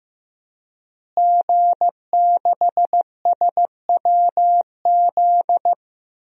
Morse code, heard as a keyed tone, about 15 wpm, 700 Hz. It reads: G6SWZ
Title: DXCC-100 Call Signs - 5 Characters - Repeated 2x - 15wpm